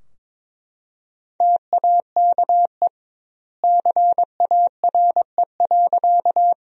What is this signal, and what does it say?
Morse code, heard as a keyed tone, about 22 wpm, 700 Hz. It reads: TAKE CARE.
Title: 2x Word-Spacing - Sentences from Top 300 Words - 22wpm